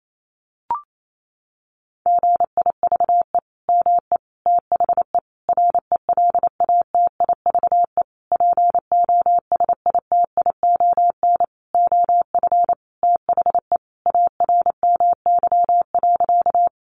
Morse code, heard as a keyed tone, about 28 wpm, 700 Hz. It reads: GIVE ME THE RELATIVE POSITION OF THE ARMY.
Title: Sentences from Top 1400 Words 28wpm